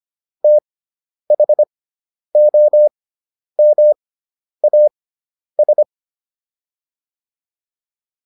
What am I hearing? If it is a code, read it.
Morse code, heard as a keyed tone, about 25 wpm, spaced as for 10 wpm, 600 Hz. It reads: THOMAS
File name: QSO306 10